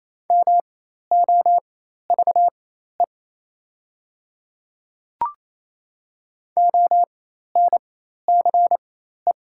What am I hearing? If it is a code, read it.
Morse code, heard as a keyed tone, about 28 wpm, spaced as for 13 wpm, 700 Hz. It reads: MOVE ONCE
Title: 4x Character Spacing - Top 500 Words - 28wpm